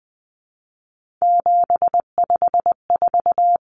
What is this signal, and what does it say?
Morse code, heard as a keyed tone, about 20 wpm, 700 Hz. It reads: 754
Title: Common Cut Numbers - 3-Digit 20wpm